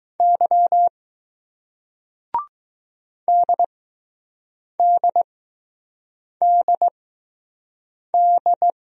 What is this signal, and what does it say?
Morse code, heard as a keyed tone, about 20 wpm, spaced as for 6 wpm, 700 Hz. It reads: Y DDDD